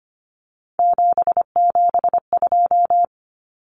Morse code, heard as a keyed tone, about 25 wpm, 700 Hz. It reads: 772
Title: Numbers - 3-Digit 25wpm